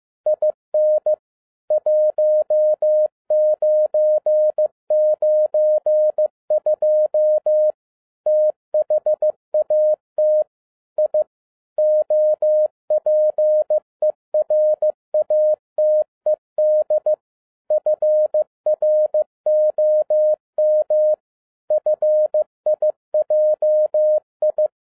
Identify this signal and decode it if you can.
Morse code, heard as a keyed tone, about 15 wpm, 610 Hz. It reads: IN 1992 THAT I OPERATED FROM FIJI
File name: 2011-10-mp3 / n07-A DOT 15wpm.mp3